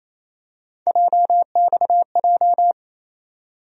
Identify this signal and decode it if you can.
Morse code, heard as a keyed tone, about 28 wpm, 700 Hz. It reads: JXJ
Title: USA Call Sign Postfixes - 3 Letters 28wpm